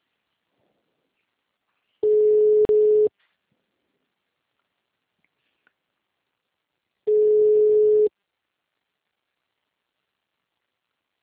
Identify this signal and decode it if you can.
Ringback tone (Europe, 425 Hz).